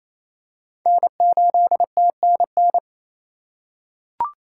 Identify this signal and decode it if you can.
Morse code, heard as a keyed tone, about 28 wpm, 700 Hz. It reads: N8TNN
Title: USA Call Signs - 5 Characters - 2x Character Spacing - 28wpm